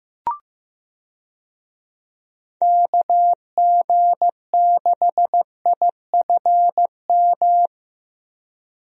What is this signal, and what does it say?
Morse code, heard as a keyed tone, about 15 wpm, 700 Hz. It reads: KG6IFM
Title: USA Call Signs - 6 Characters - Repeated 2x - 15wpm